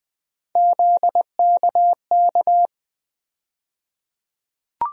Morse code, heard as a keyed tone, about 20 wpm, 700 Hz. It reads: ZKK
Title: USA Call Sign Postfixes - 3 Letters 20wpm